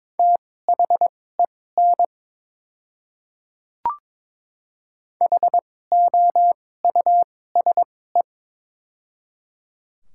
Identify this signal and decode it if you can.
Morse code, heard as a keyed tone, about 23 wpm, spaced as for 16 wpm, 690 Hz. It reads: THEN HOUSE
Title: 2x Character Spacing - Top 200 Words - 22wpm